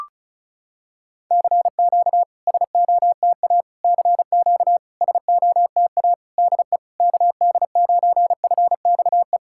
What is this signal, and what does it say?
Morse code, heard as a keyed tone, about 36 wpm, 700 Hz. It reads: CQ SOTA CQ SOTA DE KD9FXE
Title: SOTA QSO Practice 35wpm